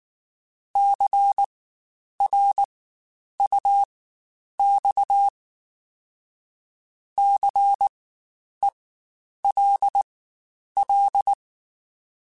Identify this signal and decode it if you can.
Morse code, heard as a keyed tone, about 19 wpm, spaced as for 9 wpm, 800 Hz. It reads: CRUX CELL
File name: Spaced Alphas / Spaced Alphas mp3s / Cs sp @18